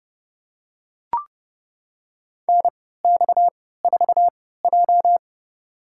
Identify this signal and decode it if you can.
Morse code, heard as a keyed tone, about 30 wpm, spaced as for 17 wpm, 700 Hz. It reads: NX4J
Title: USA Call Signs - 4 Characters - 3x Character Spacing - 30wpm